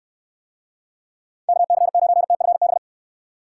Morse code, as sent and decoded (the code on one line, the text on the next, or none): -.- -.-. ----. - ..-. -...
KC9TFB